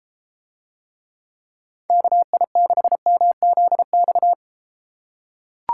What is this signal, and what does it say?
Morse code, heard as a keyed tone, about 33 wpm, 700 Hz. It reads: KI6MZX